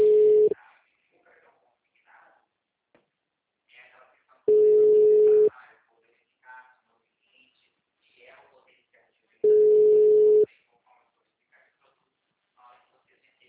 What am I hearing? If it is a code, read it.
Ringback tone (Europe, 425 Hz), with speech.
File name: out- -2046- - - .11799